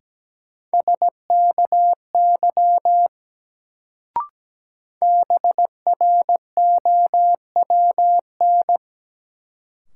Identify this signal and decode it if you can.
Morse code, heard as a keyed tone, about 17 wpm, 700 Hz. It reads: SKY BROWN
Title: Sentences from Top 1200 Words - New Word Drill - 17wpm